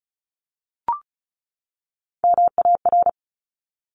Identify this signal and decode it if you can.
Morse code, heard as a keyed tone, about 36 wpm, 700 Hz. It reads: MAR
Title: ARRL Sections 35wpm